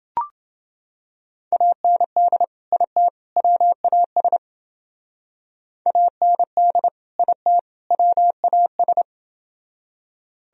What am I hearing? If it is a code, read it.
Morse code, heard as a keyed tone, about 29 wpm, 700 Hz. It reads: AND IT WAS AND IT WAS